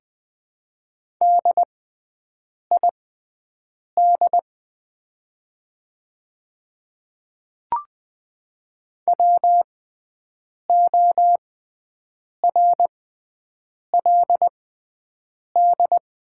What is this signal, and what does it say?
Morse code, heard as a keyed tone, about 20 wpm, spaced as for 7 wpm, 700 Hz. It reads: DID WORLD
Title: 6x Character Spacing - Top 300 Words - 20wpm